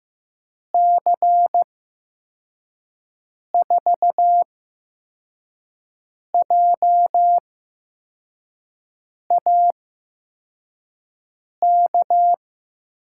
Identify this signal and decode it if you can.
Morse code, heard as a keyed tone, about 15 wpm, spaced as for 4 wpm, 700 Hz. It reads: C4JAK
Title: USA Call Signs - 6 Characters - 8x Character Spacing - 15wpm